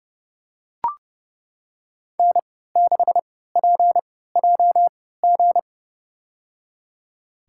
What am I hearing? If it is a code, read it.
Morse code, heard as a keyed tone, about 32 wpm, spaced as for 17 wpm, 700 Hz. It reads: N6PJG
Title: USA Call Signs - 5 Characters - 3x Character Spacing - 30wpm